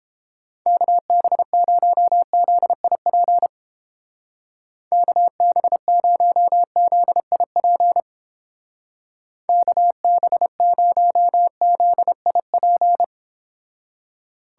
Morse code, heard as a keyed tone, about 29 wpm, 700 Hz. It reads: KB0ZIP KB0ZIP KB0ZIP